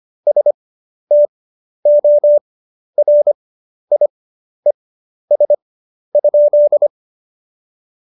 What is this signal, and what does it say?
Morse code, heard as a keyed tone, about 25 wpm, spaced as for 11 wpm, 590 Hz. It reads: STORIES?
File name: Jokes-25-char-10-eff